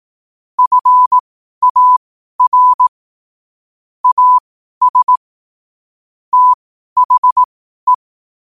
Morse code, heard as a keyed tone, about 18 wpm, spaced as for 12 wpm, 1000 Hz. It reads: FAR AS THE
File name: qotdfiles / 12wpm